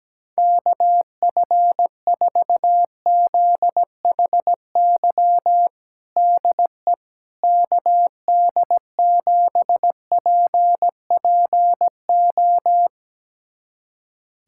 Morse code, heard as a keyed tone, about 17 wpm, 700 Hz. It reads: KF4ZHY DE KD7PPO
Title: USA Callsign DE Callsign 17wpm